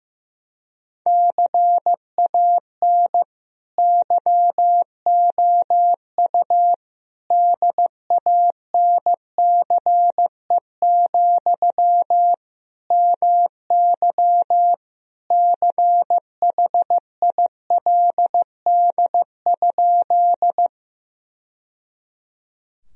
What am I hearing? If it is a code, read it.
Morse code, heard as a keyed tone, about 15 wpm, 700 Hz. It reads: CAN YOU DANCE, MY CHILD?
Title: Sentences from Top 1000 Words - Review Part 4 - 15wpm